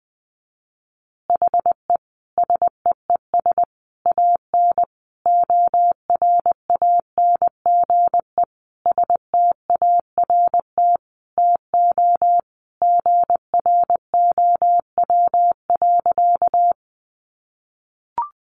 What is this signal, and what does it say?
Morse code, heard as a keyed tone, about 19 wpm, 700 Hz. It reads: HE SEES AN ORANGE START TO GROW.
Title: Easy Phrases 20wpm